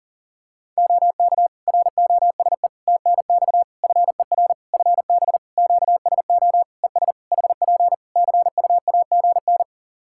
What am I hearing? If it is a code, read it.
Morse code, heard as a keyed tone, about 40 wpm, 700 Hz. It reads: OK ROSE TNX FER FB QSO ES HP CUAGN
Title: Practice QSO - Exchange 4 - Ending - 40wpm